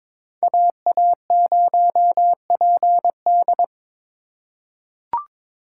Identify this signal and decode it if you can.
Morse code, heard as a keyed tone, about 22 wpm, 700 Hz. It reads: AA0PD
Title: USA Call Signs - 5 Characters - 3x Character Spacing - 22wpm